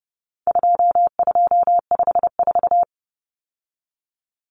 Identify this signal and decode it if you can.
Morse code, heard as a keyed tone, about 30 wpm, 700 Hz. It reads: 2254
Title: Common Cut Numbers - 4-Digit 30wpm